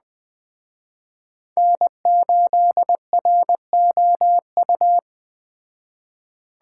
Morse code, heard as a keyed tone, about 21 wpm, 700 Hz. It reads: N8ROU
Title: Speed Racing - USA 1x3 Call Signs 17wpm